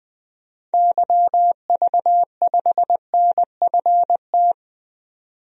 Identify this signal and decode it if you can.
Morse code, heard as a keyed tone, about 20 wpm, 700 Hz. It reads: YV5NFT
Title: DXCC-100 Call Signs - 6 Characters - 20wpm